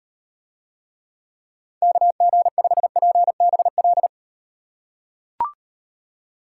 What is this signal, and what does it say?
Morse code, heard as a keyed tone, about 38 wpm, 700 Hz. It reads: KG5PBL